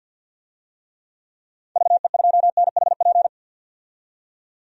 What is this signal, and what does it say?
Morse code, heard as a keyed tone, about 52 wpm, 700 Hz. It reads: VE2NHP
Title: DXCC-100 Call Signs - 6 Characters - 50wpm